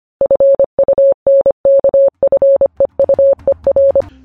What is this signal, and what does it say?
Morse code, heard as a keyed tone, about 25 wpm, 550 Hz. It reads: FUNKFEUER
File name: funkfeuer / funkfeuer-files / Q-und-Zahlencodes-Teil-4